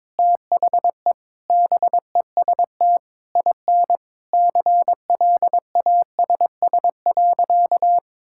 Morse code, heard as a keyed tone, about 22 wpm, 700 Hz. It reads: THE BEST IN CLASS.
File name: practice / 22wpm / Sentences with 8-Word Phrases - 22wpm